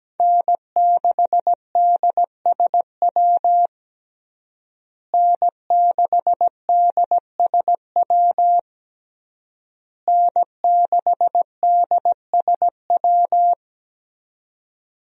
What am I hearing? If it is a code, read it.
Morse code, heard as a keyed tone, about 17 wpm, 700 Hz. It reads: N6DSW N6DSW N6DSW
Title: USA Call Signs - 5 Characters - Repeated 3x - 17wpm